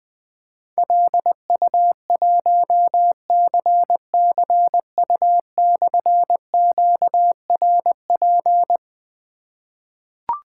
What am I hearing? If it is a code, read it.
Morse code, heard as a keyed tone, about 20 wpm, 700 Hz. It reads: LU1CCU/QRP
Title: Difficult Call Signs 20wpm